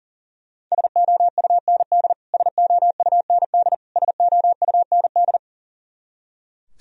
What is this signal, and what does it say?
Morse code, heard as a keyed tone, about 40 wpm, 700 Hz. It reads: SOUND SOUND SOUND
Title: Top 100 Words Repeated 40wpm